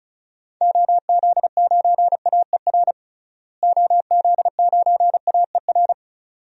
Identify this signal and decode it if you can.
Morse code, heard as a keyed tone, about 36 wpm, 700 Hz. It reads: OZ9AER OZ9AER
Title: DXCC-100 Call Signs - 6 Characters - Repeated 2x - 35wpm